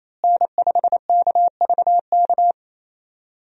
Morse code, heard as a keyed tone, about 28 wpm, 700 Hz. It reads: N5KVK
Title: USA Call Signs - 5 Characters - Repeated 3x - 28wpm